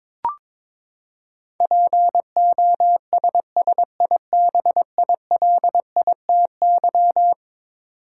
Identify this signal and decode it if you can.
Morse code, heard as a keyed tone, about 22 wpm, 700 Hz. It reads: POSSIBILITY
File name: practice / 22wpm / Eleven Letter Words - 22wpm